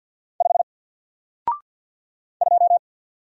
Morse code, heard as a keyed tone, about 50 wpm, 700 Hz. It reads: 5 2
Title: Single Character (Letter:Number:Punctuation:Prosign) Alt 50wpm